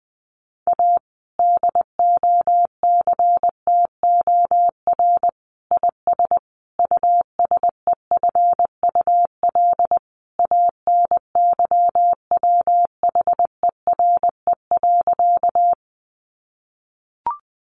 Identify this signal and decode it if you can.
Morse code, heard as a keyed tone, about 20 wpm, 700 Hz. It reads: A DOCTOR IS USEFUL ANYWHERE.
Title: Sentences from Top 1500 Words 20wpm